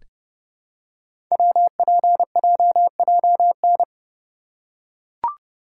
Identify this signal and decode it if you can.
Morse code, heard as a keyed tone, about 30 wpm, 700 Hz. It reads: WPJJN
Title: Code Groups - 5 Letters - 30wpm